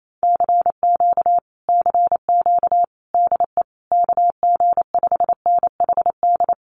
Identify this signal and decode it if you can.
Morse code, heard as a keyed tone, about 28 wpm, 700 Hz. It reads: CQ CQ DE KG5NHD
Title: Common QSO Phrases Encore 28wpm